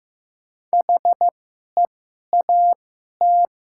Morse code, heard as a keyed tone, about 15 wpm, spaced as for 11 wpm, 700 Hz. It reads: HEAT
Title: 2x Character Spacing - Top 500 Words - 15wpm